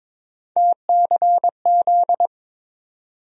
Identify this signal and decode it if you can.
Morse code, heard as a keyed tone, about 22 wpm, 700 Hz. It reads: TCZ
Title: USA Call Sign Postfixes - 3 Letters 22wpm